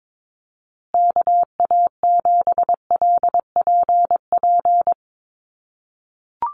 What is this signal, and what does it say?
Morse code, heard as a keyed tone, about 22 wpm, 700 Hz. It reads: KA7LPP
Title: USA Call Signs Repeated 2x 22wpm